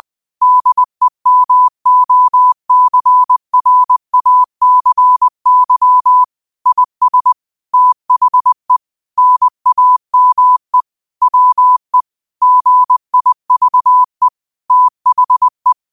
Morse code, heard as a keyed tone, about 20 wpm, 1000 Hz. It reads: DEMOCRACY IS THE NAME WE GIVE THE